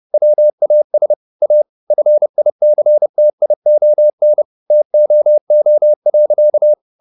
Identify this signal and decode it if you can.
Morse code, heard as a keyed tone, about 30 wpm, 600 Hz. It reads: WAS A FICTION TOO.
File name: the old man and the sea / the old man and the sea 1 30wpm